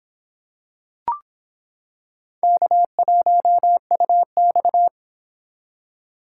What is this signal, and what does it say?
Morse code, heard as a keyed tone, about 26 wpm, 700 Hz. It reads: K1UX